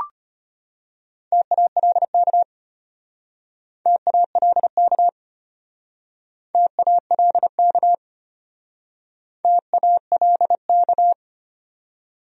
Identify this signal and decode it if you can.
Morse code, heard as a keyed tone, about 33 wpm, 700 Hz. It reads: TALK TALK TALK TALK